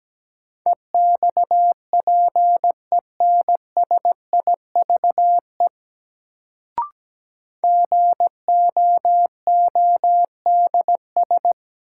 Morse code, heard as a keyed tone, about 17 wpm, 700 Hz. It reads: EXPENSIVE GOODS
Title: Sentences from Top 1000 Words - Review Part 6 - New Word Drill - 17wpm